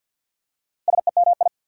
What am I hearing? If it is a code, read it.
Morse code, heard as a keyed tone, about 52 wpm, 690 Hz. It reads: SEMI
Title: Prefixes and Suffixes 50wpm